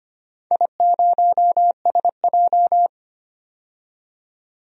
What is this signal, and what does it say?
Morse code, heard as a keyed tone, about 25 wpm, 700 Hz. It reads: I0SJ
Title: DXCC-100 Call Signs - 4 Characters - 25wpm